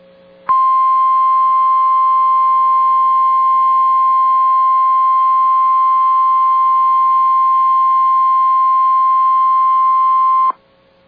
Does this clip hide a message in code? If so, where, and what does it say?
Yes: Emergency Alert System (SAME) tones (the 1050 Hz attention tone) from 0.4 to 10.5 s.